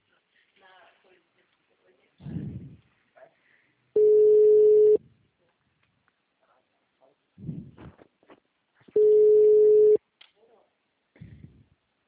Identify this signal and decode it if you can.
Ringback tone (Europe, 425 Hz), with speech.
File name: out- -2002- - - .19461